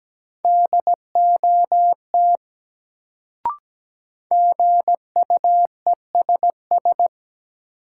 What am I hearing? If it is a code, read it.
Morse code, heard as a keyed tone, about 17 wpm, 700 Hz. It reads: DOT GUESS
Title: Sentences from Top 1500 Words - New Word Drill - 17wpm